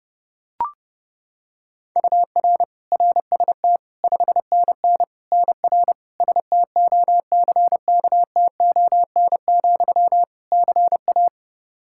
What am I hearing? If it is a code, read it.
Morse code, heard as a keyed tone, about 32 wpm, 700 Hz. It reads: UR RST 5NN NR STOCKTON, CA
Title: Practice QSO - Exchange 1 - Beginning - 30wpm